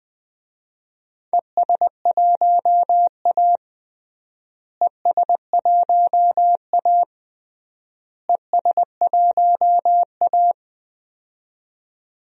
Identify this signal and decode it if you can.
Morse code, heard as a keyed tone, about 20 wpm, 700 Hz. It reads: ES1A ES1A ES1A
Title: DXCC-100 Call Signs - 4 Characters - Repeated 3x - 20wpm